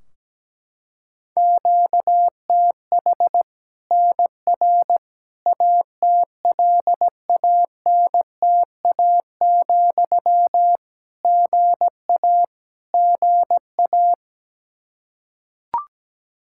Morse code, heard as a keyed tone, about 17 wpm, 700 Hz. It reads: QTH NR ATLANTA, GA GA
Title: K1USN SST QSO Practice 17wpm